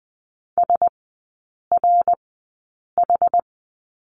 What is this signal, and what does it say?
Morse code, heard as a keyed tone, about 20 wpm, spaced as for 8 wpm, 700 Hz. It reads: SRH
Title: 145 Lesson 24e - Letter y - Three Characters 20wpm